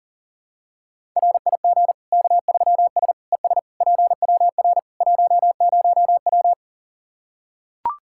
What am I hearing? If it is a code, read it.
Morse code, heard as a keyed tone, about 40 wpm, 700 Hz. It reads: RIG K3S ES PWR 10W
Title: Practice QSO - Exchange 2 - Info - 40wpm